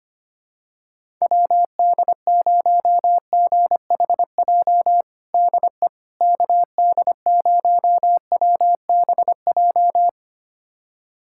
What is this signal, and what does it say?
Morse code, heard as a keyed tone, about 25 wpm, 700 Hz. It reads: WD0GHJ DE KD0WBJ
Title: USA Callsign DE Callsign - 6 Characters - 25wpm